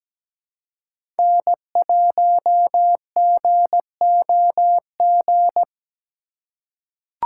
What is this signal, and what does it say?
Morse code, heard as a keyed tone, about 17 wpm, 700 Hz. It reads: N1GOG
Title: USA Call Signs - 3x Character Spacing - 17wpm